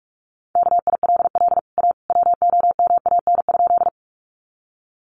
Morse code, heard as a keyed tone, about 44 wpm, 700 Hz. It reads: KILL A WOMAN?
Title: Sentences from Top 1400 Words 45wpm